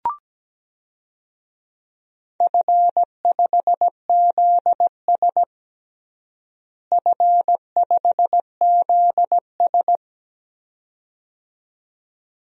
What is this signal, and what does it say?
Morse code, heard as a keyed tone, about 17 wpm, 700 Hz. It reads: F5ZS F5ZS
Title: DXCC-100 Call Signs - 4 Characters - Repeated 2x - 17wpm